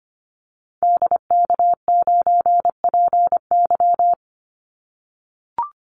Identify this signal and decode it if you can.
Morse code, heard as a keyed tone, about 25 wpm, 700 Hz. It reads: DK9PY
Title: Call Signs - 25wpm